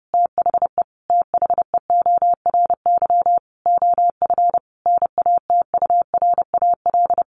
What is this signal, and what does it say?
Morse code, heard as a keyed tone, about 30 wpm, 700 Hz. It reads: THE THEORY OF NATURAL
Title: Sentences from Top 1400 Words 30wpm